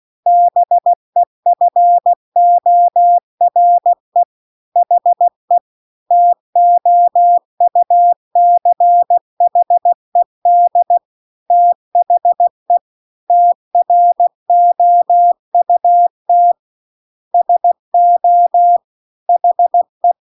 Morse code, heard as a keyed tone, about 16 wpm, 700 Hz. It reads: BEFORE HE TOUCHED THE TROUT SO HE
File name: CW-The-Big-Twoheaded-River-2-16WPM